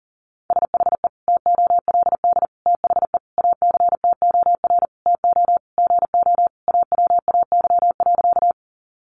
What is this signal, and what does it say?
Morse code, heard as a keyed tone, about 40 wpm, 700 Hz. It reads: SHE TOLD THE ACTOR TO GO AWAY.
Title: Sentences from Top 1500 Words 40wpm